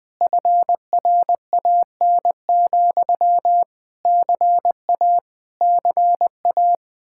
Morse code, heard as a keyed tone, about 20 wpm, 700 Hz. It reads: FRAN, CA CA
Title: Speed-Racing Common QSO Phrases - 17wpm